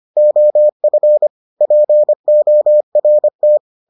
Morse code, heard as a keyed tone, about 25 wpm, 600 Hz. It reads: OF PORT